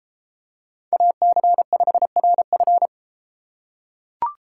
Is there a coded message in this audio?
Yes: Morse code from 0.9 to 2.9 s, heard as a keyed tone, about 32 wpm, 700 Hz. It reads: AC5RF